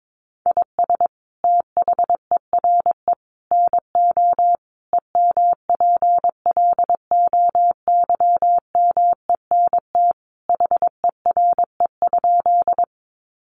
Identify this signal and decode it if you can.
Morse code, heard as a keyed tone, about 22 wpm, 700 Hz. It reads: IS THERE NO EMPLOYMENT HERE?
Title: Sentences from Top 1500 Words 22wpm